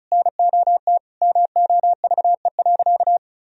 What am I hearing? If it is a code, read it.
Morse code, heard as a keyed tone, about 34 wpm, 700 Hz. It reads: NOT MOVE.
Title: Sentences from Top 1000 Words - Review Part 1 - 35wpm